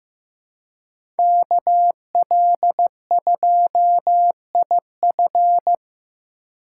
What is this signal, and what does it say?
Morse code, heard as a keyed tone, about 15 wpm, 700 Hz. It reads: KL2IF
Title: USA Call Signs - 5 Characters - 4x Character Spacing - 15wpm